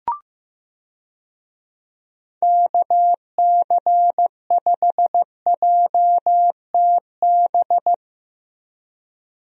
Morse code, heard as a keyed tone, about 15 wpm, 700 Hz. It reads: KC5JTB
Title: USA Call Signs - 6 Characters - Repeated 3x - 15wpm